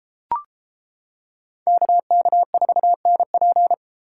Morse code, heard as a keyed tone, about 31 wpm, 700 Hz. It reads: KK4NP